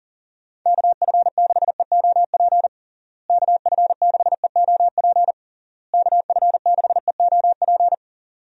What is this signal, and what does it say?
Morse code, heard as a keyed tone, about 40 wpm, 700 Hz. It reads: KF6EOP KF6EOP KF6EOP
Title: USA Call Signs - 6 Characters - Repeated 3x - 40wpm